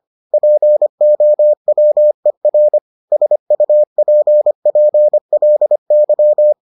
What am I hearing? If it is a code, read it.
Morse code, heard as a keyed tone, about 25 wpm, 600 Hz. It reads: POWER SUPPLY